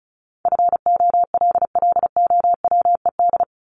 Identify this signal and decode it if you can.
Morse code, heard as a keyed tone, about 34 wpm, 700 Hz. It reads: FOLLOWED
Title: ED Suffix - Top 100 Words with Base Words 35wpm